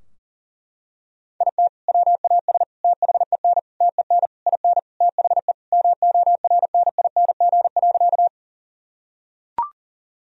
Morse code, heard as a keyed tone, about 37 wpm, 700 Hz. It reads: IT WAS THEN TEN IN THE MORNING.
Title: Sentences from Top 1000 Words - Review Part 2 - 40wpm